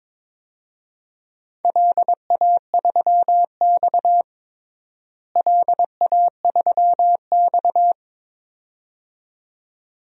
Morse code, heard as a keyed tone, about 22 wpm, 700 Hz. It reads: LA3X LA3X
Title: DXCC-100 Call Signs - 4 Characters - Repeated 2x - 22wpm